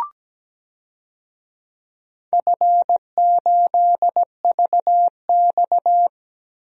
Morse code, heard as a keyed tone, about 17 wpm, 700 Hz. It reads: F8VX
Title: DXCC-100 Call Signs - 4 Characters - 17wpm